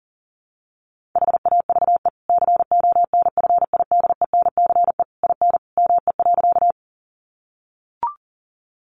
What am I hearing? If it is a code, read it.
Morse code, heard as a keyed tone, about 37 wpm, 700 Hz. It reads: HAVE CONFIDENCE IN ME.
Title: Sentences from Top 1500 Words 40wpm